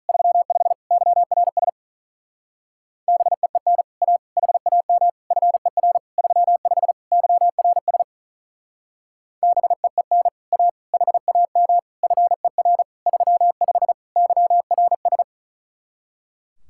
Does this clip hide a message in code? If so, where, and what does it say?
Yes: Morse code from 0.1 to 15.2 s, heard as a keyed tone, about 45 wpm, 700 Hz. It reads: 35 YRS BEEN A HAM FER 35 YRS BEEN A HAM FER 35 YRS